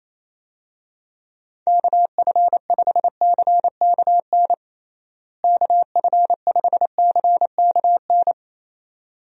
Morse code, heard as a keyed tone, about 27 wpm, 700 Hz. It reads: KF5CKN KF5CKN
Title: USA Call Signs - 6 Characters - Repeated 2x - 28wpm